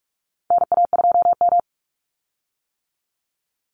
Morse code, heard as a keyed tone, about 44 wpm, 700 Hz. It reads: NA2M
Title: Call Signs - 45wpm